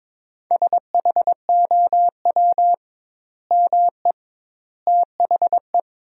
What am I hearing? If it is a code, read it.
Morse code, heard as a keyed tone, about 22 wpm, 700 Hz. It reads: SHOW ME THE
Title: 2x Word-Spacing - Sentences from Top 200 Words - 22wpm